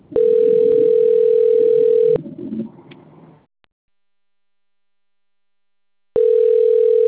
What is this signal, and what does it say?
Ringback tone (North America), 440 + 480 Hz.